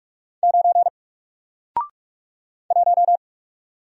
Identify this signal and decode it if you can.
Morse code, heard as a keyed tone, about 44 wpm, 700 Hz. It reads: 9 1
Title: Single Letter-Number 45wpm